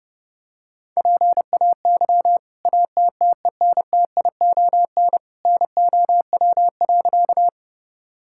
Morse code, heard as a keyed tone, about 32 wpm, 700 Hz. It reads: PAY ATTENTION NOW.
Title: Sentences from Top 1000 Words - Review Part 6 - 30wpm